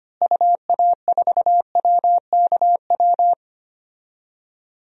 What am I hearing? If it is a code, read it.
Morse code, heard as a keyed tone, about 25 wpm, 700 Hz. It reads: UA4WKW
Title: DXCC-100 Call Signs - 6 Characters - Repeated 3x - 25wpm